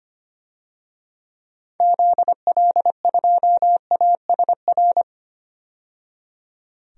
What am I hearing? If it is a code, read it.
Morse code, heard as a keyed tone, about 25 wpm, 700 Hz. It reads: ZL2ASR
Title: DXCC-100 Call Signs - 6 Characters - 25wpm